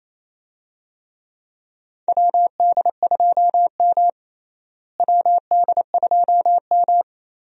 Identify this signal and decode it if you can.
Morse code, heard as a keyed tone, about 27 wpm, 700 Hz. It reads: WD2M WD2M
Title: USA Call Signs - 4 Characters - Repeated 3x - 28wpm